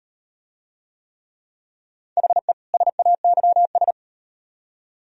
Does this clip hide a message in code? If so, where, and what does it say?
Yes: Morse code from 2.2 to 3.9 s, heard as a keyed tone, about 38 wpm, 700 Hz. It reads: HE SAYS